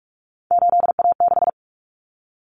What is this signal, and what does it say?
Morse code, heard as a keyed tone, about 44 wpm, 700 Hz. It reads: 8A6
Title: Common Cut Numbers - 3-Digit 45wpm